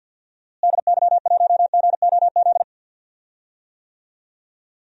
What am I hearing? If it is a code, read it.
Morse code, heard as a keyed tone, about 51 wpm, 700 Hz. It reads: DY1GOZ
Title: DXCC-100 Call Signs - 6 Characters - Repeated 2x - 50wpm